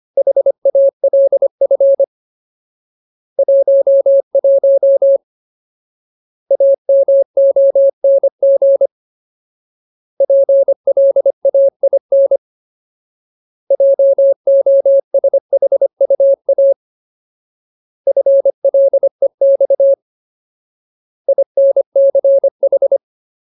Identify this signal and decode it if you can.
Morse code, heard as a keyed tone, about 25 wpm, 550 Hz. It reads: HALF 11 AMONG PLAIN JOSHUA FLEX INCH
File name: Ultimate Word List / 25WPM Ultimate